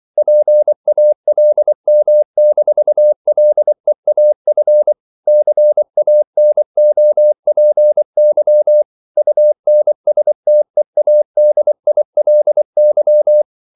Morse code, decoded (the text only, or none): PALM-LEAF CANOPY UNSTEADILY